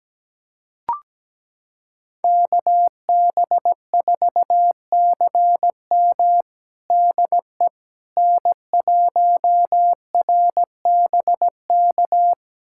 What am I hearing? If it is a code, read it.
Morse code, heard as a keyed tone, about 17 wpm, 700 Hz. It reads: KB4CM DE N1RBK
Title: USA Callsign DE Callsign - 5 Characters - 17wpm